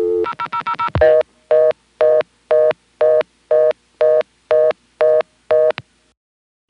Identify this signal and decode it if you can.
Reorder tone (North America), 480 + 620 Hz.